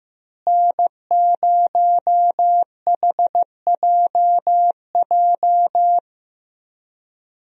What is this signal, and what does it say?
Morse code, heard as a keyed tone, about 15 wpm, 700 Hz. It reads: N0HJJ
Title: USA Call Signs - 5 Characters - Repeated 2x - 15wpm